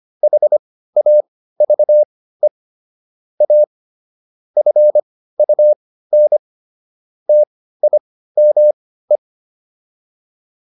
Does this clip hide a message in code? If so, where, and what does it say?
Yes: Morse code from 0.2 to 9.2 s, heard as a keyed tone, about 25 wpm, spaced as for 15 wpm, 600 Hz. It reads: HAVE A FUN TIME